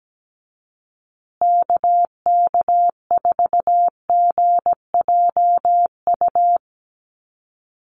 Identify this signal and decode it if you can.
Morse code, heard as a keyed tone, about 17 wpm, 700 Hz. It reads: KK4GJU
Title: USA Call Signs Repeated 3x 17wpm